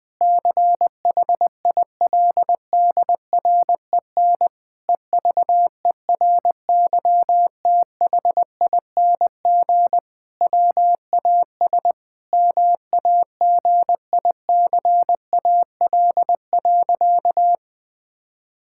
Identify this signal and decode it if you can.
Morse code, heard as a keyed tone, about 20 wpm, 700 Hz. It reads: CHILDREN EVERYTHING WAS MAGICAL.
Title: Sentences with 5-Word Phrases - Spoken Beginning - 20wpm